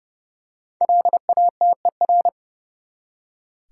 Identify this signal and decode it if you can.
Morse code, heard as a keyed tone, about 30 wpm, 700 Hz. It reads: LATER
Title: Five Letter Words - 30wpm